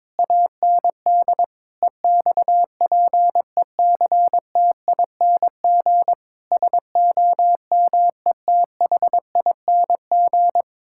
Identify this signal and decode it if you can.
Morse code, heard as a keyed tone, about 22 wpm, 700 Hz. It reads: AND EXPECTING SOMETHING
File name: practice / 22wpm / Sentences with 8-Word Phrases - 22wpm